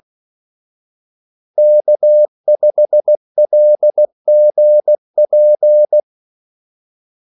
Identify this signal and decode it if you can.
Morse code, heard as a keyed tone, about 16 wpm, 600 Hz. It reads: K5LGP
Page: Kaldesignaler 16 wpm